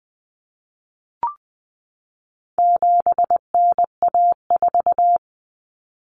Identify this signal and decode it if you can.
Morse code, heard as a keyed tone, about 20 wpm, 700 Hz. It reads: 7NA4
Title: Common Cut Numbers - 4-Digit 20wpm